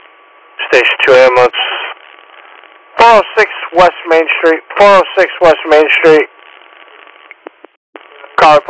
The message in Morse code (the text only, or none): none